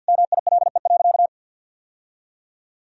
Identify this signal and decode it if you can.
Morse code, heard as a keyed tone, about 52 wpm, 700 Hz. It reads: MILE.
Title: Sentences from Top 1000 Words - Review Part 4 - 50wpm